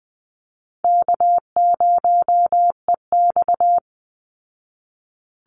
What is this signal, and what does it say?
Morse code, heard as a keyed tone, about 20 wpm, 700 Hz. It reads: K0EX
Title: Easy Call Signs Repeated 2x - 20wpm